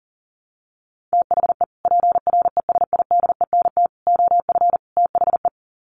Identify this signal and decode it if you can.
Morse code, heard as a keyed tone, about 40 wpm, 700 Hz. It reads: THE PRESIDENT OF THE
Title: Sets of 4 Words 40wpm